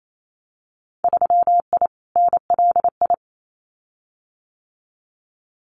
Morse code, heard as a keyed tone, about 28 wpm, 700 Hz. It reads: 3I NLI
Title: Winter Field Day Exchanges Encore 28wpm